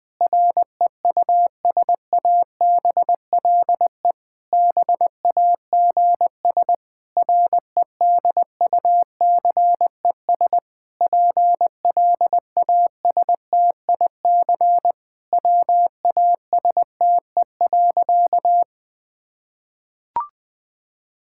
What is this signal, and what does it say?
Morse code, heard as a keyed tone, about 20 wpm, 700 Hz. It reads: REUSABLE BAGS REDUCES PLASTIC WASTE.
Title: Difficult Sentences with Hints - 20wpm